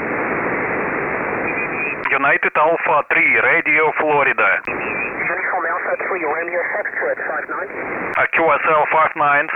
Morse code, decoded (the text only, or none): none